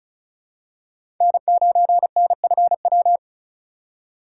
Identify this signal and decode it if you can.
Morse code, heard as a keyed tone, about 34 wpm, 700 Hz. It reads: N9NFW
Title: USA Call Signs - 2x Character Spacing - 35wpm